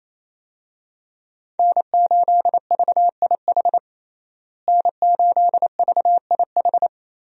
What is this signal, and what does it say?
Morse code, heard as a keyed tone, about 28 wpm, 700 Hz. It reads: N8VIH N8VIH
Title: USA Call Signs - 5 Characters - Repeated 2x - 28wpm